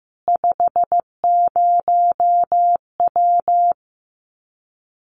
Morse code, heard as a keyed tone, about 15 wpm, 700 Hz. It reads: 50W
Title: Common QSO Phrases 15wpm